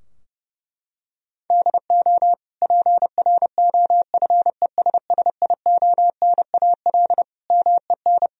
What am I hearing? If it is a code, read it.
Morse code, heard as a keyed tone, about 30 wpm, 700 Hz. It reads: DO PROFESSIONAL MEN
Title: Sentences from Top 1000 Words - Review Part 4 - 30wpm